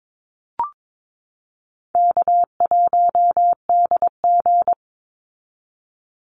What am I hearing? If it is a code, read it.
Morse code, heard as a keyed tone, about 22 wpm, 700 Hz. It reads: K1DG
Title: Easy Call Signs Repeated 3x - 22wpm